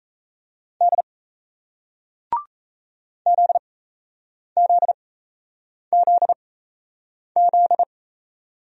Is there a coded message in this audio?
Yes: Morse code from 0.8 to 7.8 s, heard as a keyed tone, about 38 wpm, spaced as for 8 wpm, 700 Hz. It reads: D ZZZZ